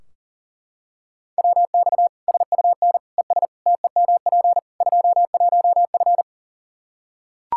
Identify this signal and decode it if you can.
Morse code, heard as a keyed tone, about 37 wpm, 700 Hz. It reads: WX SUN ES TEMP 21F
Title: Practice QSO - Exchange 2 - Info - 40wpm